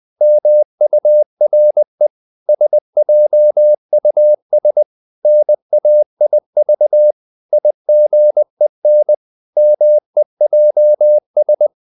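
Morse code, heard as a keyed tone, about 20 wpm, 600 Hz. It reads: MURE SJUS NAIV IGEN MEJS
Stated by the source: Morse code practice of short Danish words at 20 wpm, from CW med Gnister.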